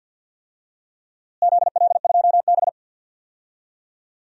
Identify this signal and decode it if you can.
Morse code, heard as a keyed tone, about 52 wpm, 700 Hz. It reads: ZL2B
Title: DXCC-100 Call Signs - 4 Characters - Repeated 2x - 50wpm